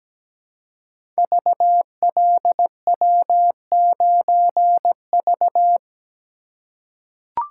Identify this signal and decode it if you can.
Morse code, heard as a keyed tone, about 17 wpm, 700 Hz. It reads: VLW9V
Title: Code Groups - 5 Letter:Numbers - 17wpm